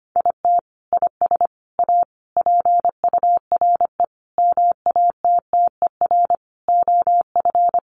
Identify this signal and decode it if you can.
Morse code, heard as a keyed tone, about 25 wpm, 700 Hz. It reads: IT IS A PURE MATTER OF
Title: Sentences from Top 1500 Words 25wpm